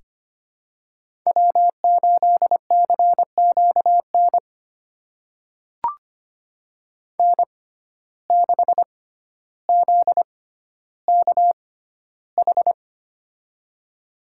Morse code, heard as a keyed tone, about 25 wpm, spaced as for 13 wpm, 700 Hz. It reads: W8CQN N6ZKH
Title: USA Call Signs - 5 Characters - 6x Character Spacing - 25wpm